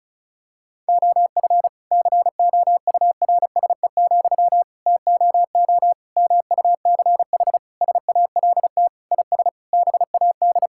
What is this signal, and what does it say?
Morse code, heard as a keyed tone, about 34 wpm, 700 Hz. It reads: OF COURSE, TOO MUCH SALT IS BAD
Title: Sentences from Top 1000 Words - Review Part 5 - 35wpm